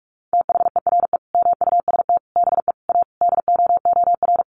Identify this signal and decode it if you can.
Morse code, heard as a keyed tone, about 44 wpm, 700 Hz. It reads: THERE MUST BE A DOOR
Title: Sentences from Top 1400 Words 45wpm